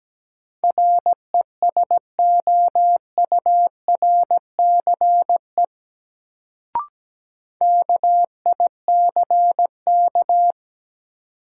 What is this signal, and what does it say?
Morse code, heard as a keyed tone, about 17 wpm, 700 Hz. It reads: RESOURCE KICK
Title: Sentences from Top 1300 Words - New Word Drill Alt - 17wpm